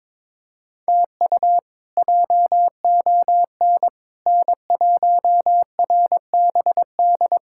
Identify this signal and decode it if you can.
Morse code, heard as a keyed tone, about 22 wpm, 700 Hz. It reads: TU JON N1RBD
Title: K1USN SST QSO Practice - 22wpm